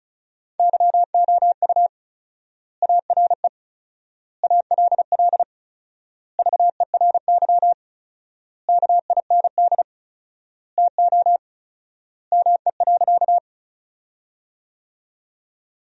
Morse code, heard as a keyed tone, about 36 wpm, 700 Hz. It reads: YOU ARE ALL VERY KIND TO ME.
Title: 4x Word-Spacing - Sentences from Top 200 Words - 35wpm